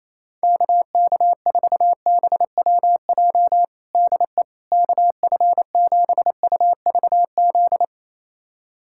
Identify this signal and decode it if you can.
Morse code, heard as a keyed tone, about 28 wpm, 700 Hz. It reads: KK4BWJ DE KF7UVZ
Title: USA Callsign DE Callsign - 6 Characters - 28wpm